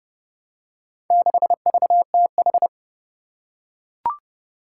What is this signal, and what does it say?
Morse code, heard as a keyed tone, about 30 wpm, 700 Hz. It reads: 6VTH
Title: Four Challenging Letter-Number Combinations 30wpm